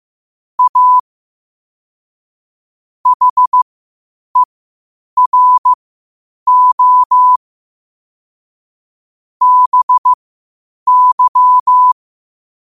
Morse code, heard as a keyed tone, about 15 wpm, spaced as for 8 wpm, 1000 Hz. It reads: A HERO BY
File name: qotdfiles / 8wpm